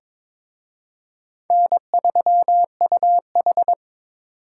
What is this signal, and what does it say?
Morse code, heard as a keyed tone, about 26 wpm, 700 Hz. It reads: N3UH
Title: Speed Racing - USA 1x2 Call Signs 22wpm